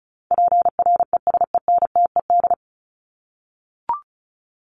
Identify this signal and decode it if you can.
Morse code, heard as a keyed tone, about 34 wpm, 700 Hz. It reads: PRESENTED
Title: ED Suffix - Top 100 Words with Base Words 35wpm